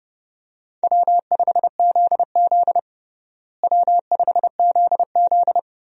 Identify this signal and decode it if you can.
Morse code, heard as a keyed tone, about 32 wpm, 700 Hz. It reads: W5ZZ W5ZZ
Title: USA Call Signs - 4 Characters - Repeated 3x - 30wpm